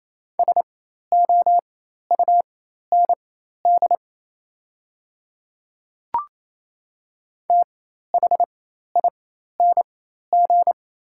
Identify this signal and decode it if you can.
Morse code, heard as a keyed tone, about 28 wpm, spaced as for 13 wpm, 690 Hz. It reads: SOUND THING
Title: 4x Character Spacing - Top 100 Words - 28wpm